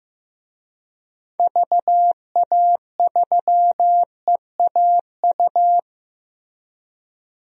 Morse code, heard as a keyed tone, about 15 wpm, 700 Hz. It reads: VA3EAU
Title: DXCC-100 Call Signs - 6 Characters - 15wpm